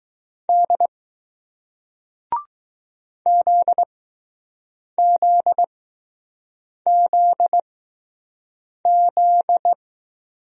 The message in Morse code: D ZZZZ

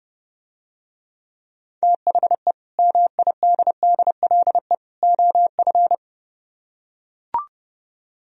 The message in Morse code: THE MIDDLE OF

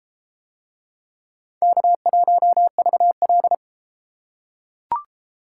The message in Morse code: K1VL

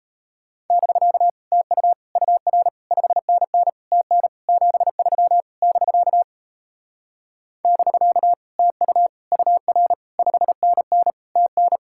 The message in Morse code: <BK> TU UR 5NN TN 73 <BK> <BK> TU UR 5NN TN

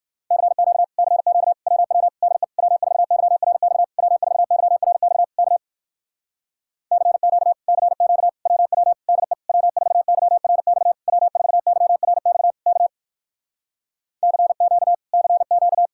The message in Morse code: CQ CQ WW DE W4YRX W4YRX K CQ CQ WW DE W4YRX W4YRX K CQ CQ

